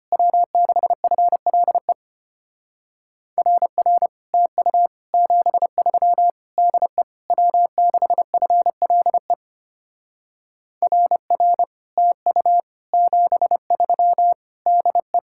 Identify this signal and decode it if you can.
Morse code, heard as a keyed tone, about 32 wpm, 700 Hz. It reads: W6FLE RR TU 73 DE W6FLE RR TU 73 DE